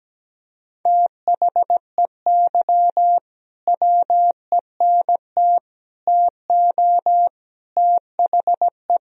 Morse code, heard as a keyed tone, about 17 wpm, 700 Hz. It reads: THEY WENT TO THE
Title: Sentences from Top 1000 Words - Review Part 4 - 17wpm